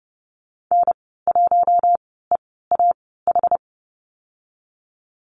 Morse code, heard as a keyed tone, about 30 wpm, spaced as for 17 wpm, 700 Hz. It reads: N1EAH
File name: Lesson 14g - Number 2 - Call Signs 30wpm